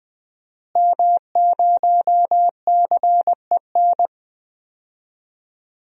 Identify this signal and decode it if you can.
Morse code, heard as a keyed tone, about 20 wpm, 700 Hz. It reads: M0CEN
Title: DXCC-100 Call Signs - 5 Characters - Repeated 2x - 20wpm